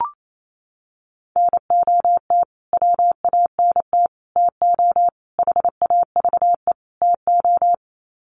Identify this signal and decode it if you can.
Morse code, heard as a keyed tone, about 28 wpm, 700 Hz. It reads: NOT WANT TO HAVE TO
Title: Sets of 5 Words Encore2 28wpm